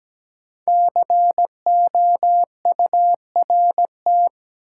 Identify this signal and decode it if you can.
Morse code, heard as a keyed tone, about 17 wpm, 700 Hz. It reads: COURT
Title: Sentences from Top 1600 Words - New Word Drill Alt - 17wpm